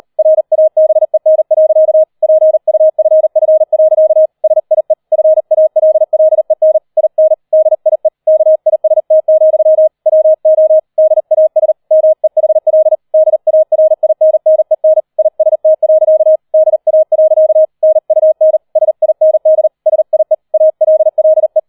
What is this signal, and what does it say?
Morse code, heard as a keyed tone, about 38 wpm, 630 Hz. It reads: RABEN. PUFF. SIE FALLEN IN DIE KIST, WO DAS MEHL DARINNEN IST. DA. NUN SIND SIE ALLE